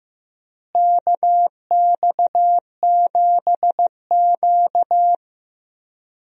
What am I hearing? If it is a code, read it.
Morse code, heard as a keyed tone, about 15 wpm, 700 Hz. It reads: KX7Q
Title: USA Call Signs - 4 Characters - Repeated 3x - 15wpm